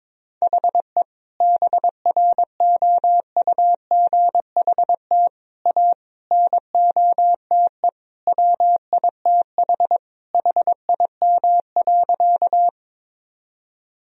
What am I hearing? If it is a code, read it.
Morse code, heard as a keyed tone, about 22 wpm, 700 Hz. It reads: HE BROUGHT A NOTE WITH HIM.
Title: Sentences from Top 1000 Words - Review Part 2 - 22wpm